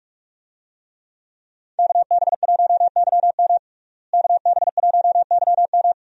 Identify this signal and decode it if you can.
Morse code, heard as a keyed tone, about 46 wpm, 700 Hz. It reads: KB1YM KB1YM
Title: USA Call Signs - 5 Characters - Repeated 2x - 45wpm